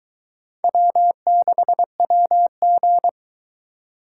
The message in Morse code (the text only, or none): W6WG